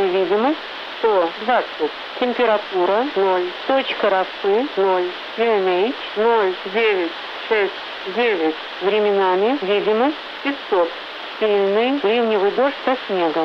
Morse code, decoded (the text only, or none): none